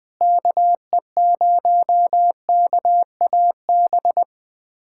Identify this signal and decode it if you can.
Morse code, heard as a keyed tone, about 21 wpm, 700 Hz. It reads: KE0KAB
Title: USA Callsign DE Callsign 20wpm